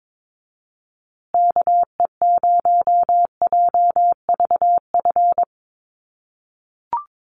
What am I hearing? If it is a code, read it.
Morse code, heard as a keyed tone, about 22 wpm, 700 Hz. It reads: KE0JVF
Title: USA Call Signs Repeated 2x 22wpm